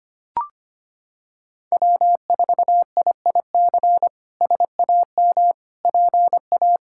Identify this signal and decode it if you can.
Morse code, heard as a keyed tone, about 25 wpm, 700 Hz. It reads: W4IIC SAM PA
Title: K1USN SST QSO Practice - 25wpm